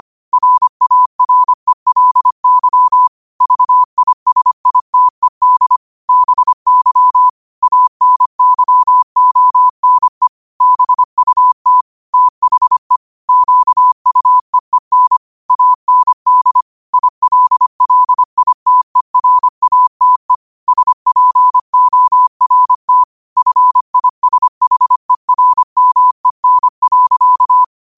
Morse code, heard as a keyed tone, about 25 wpm, 1000 Hz. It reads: RARELY VISITED BY ANYONE BUT THE QUEEN AND ILLITERATE SPORT FISHERMEN.
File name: qotdfiles / 25wpm